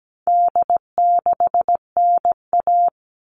Morse code, heard as a keyed tone, about 17 wpm, 700 Hz. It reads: D6NA
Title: All Cut Numbers - 4-Digit 17wpm